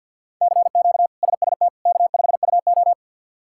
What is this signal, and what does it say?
Morse code, heard as a keyed tone, about 46 wpm, 700 Hz. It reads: CQ SST K5UO
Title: K1USN SST QSO Practice - 50wpm